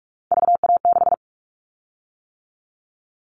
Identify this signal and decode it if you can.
Morse code, heard as a keyed tone, about 46 wpm, 700 Hz. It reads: VA6
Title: Callsign Prefixes for Top 50 DXCC - 45wpm